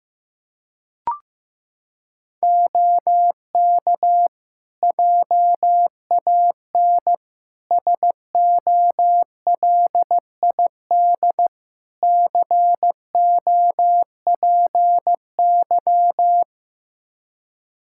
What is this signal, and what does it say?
Morse code, heard as a keyed tone, about 15 wpm, 700 Hz. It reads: OK JAN SOLID COPY
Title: K1USN SST QSO Practice 15wpm